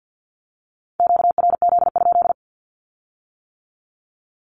Morse code, heard as a keyed tone, about 50 wpm, 700 Hz. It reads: QRZ?
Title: Q-Codes with Spoken Meaning - 50wpm